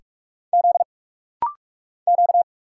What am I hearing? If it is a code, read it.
Morse code, heard as a keyed tone, about 44 wpm, 700 Hz. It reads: Z Q
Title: Single Letters - Rapid-Fire 45wpm